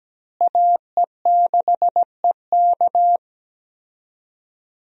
Morse code, heard as a keyed tone, about 17 wpm, 700 Hz. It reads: AE6EK
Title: USA Call Signs - 5 Characters - Repeated 3x - 17wpm